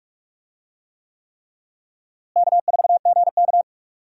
Morse code, heard as a keyed tone, about 46 wpm, 700 Hz. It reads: K4GK
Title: USA Call Signs - 4 Characters - Repeated 3x - 45wpm